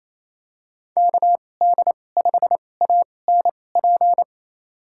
Morse code, heard as a keyed tone, about 28 wpm, spaced as for 20 wpm, 700 Hz. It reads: KD5ANP
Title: USA Call Signs - 6 Characters - 2x Character Spacing - 28wpm